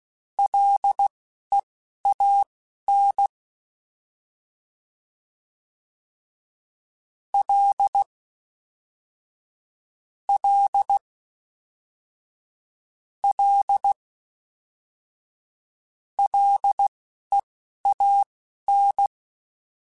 Morse code, heard as a keyed tone, about 16 wpm, spaced as for 12 wpm, 800 Hz. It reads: LEAN L L L LEAN